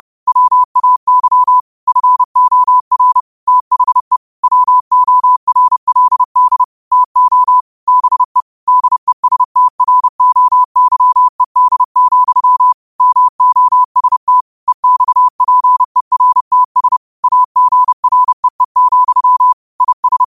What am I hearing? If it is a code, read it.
Morse code, heard as a keyed tone, about 30 wpm, 1000 Hz. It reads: WAY FOR THE WORLD TO BE DESTROYED, MOST EXPERTS AGREE, IS